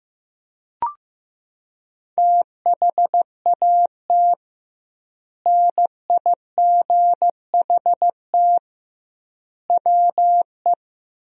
Morse code, heard as a keyed tone, about 16 wpm, 700 Hz. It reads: THAT NIGHT WE
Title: 2x Word-Spacing - Sentences from Top 300 Words - 15wpm